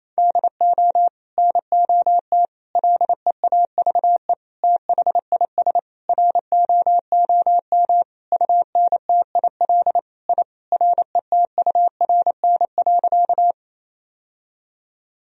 Morse code, heard as a keyed tone, about 28 wpm, 700 Hz. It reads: DO NOT LEAVE THIS ROOM UNTIL I RETURN.
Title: Sentences from Top 1000 Words - Review Part 1 - 28wpm